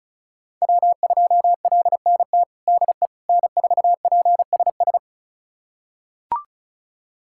Morse code, heard as a keyed tone, about 36 wpm, 700 Hz. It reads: W2LNT DE N4PSS
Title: USA Callsign DE Callsign - 5 Characters - 35wpm